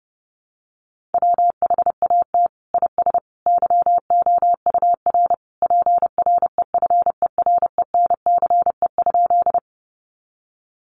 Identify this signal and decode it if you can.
Morse code, heard as a keyed tone, about 30 wpm, 700 Hz. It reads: WHAT IS YOUR PREFERENCE?
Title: Sentences from Top 1500 Words 30wpm